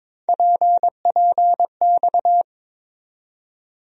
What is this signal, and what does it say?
Morse code, heard as a keyed tone, about 22 wpm, 700 Hz. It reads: PPX
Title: USA Call Sign Postfixes - 3 Letters 22wpm